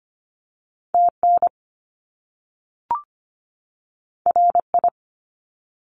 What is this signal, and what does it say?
Morse code, heard as a keyed tone, about 25 wpm, 700 Hz. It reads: TN RI
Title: US States 25wpm